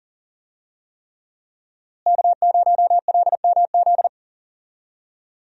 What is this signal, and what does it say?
Morse code, heard as a keyed tone, about 40 wpm, 700 Hz. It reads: K0LMZ
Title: USA Call Signs - 5 Characters - Repeated 3x - 40wpm